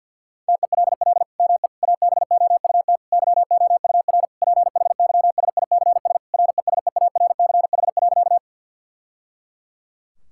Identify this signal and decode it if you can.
Morse code, heard as a keyed tone, about 50 wpm, 700 Hz. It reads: TELL ME ABOUT YOUR PHYSICS RESEARCH.
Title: Sentences from Top 1000 Words - Review Part 5 - 50wpm